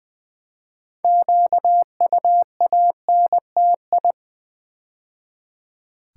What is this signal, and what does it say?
Morse code, heard as a keyed tone, about 20 wpm, 700 Hz. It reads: QUANTI
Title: Build A Word - 2 Letters At A Time - 8-Letter Words - 20wpm